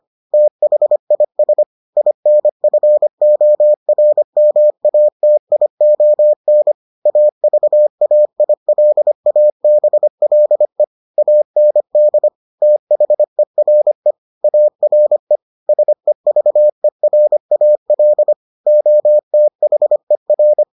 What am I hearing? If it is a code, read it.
Morse code, heard as a keyed tone, about 25 wpm, 600 Hz. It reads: THIS INFORMATION AVAILABLE AND THERE ARE SEVERAL OTHER